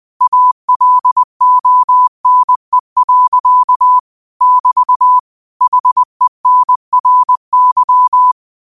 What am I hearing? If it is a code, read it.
Morse code, heard as a keyed tone, about 20 wpm, 1000 Hz. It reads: ALONE. <BT> HENRY